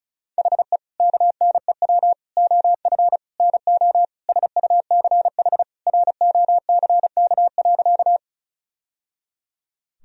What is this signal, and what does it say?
Morse code, heard as a keyed tone, about 36 wpm, 700 Hz. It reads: HE KNEW OF NO SUCH ROCK.
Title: Sentences from Top 1000 Words - Review Part 2 - 35wpm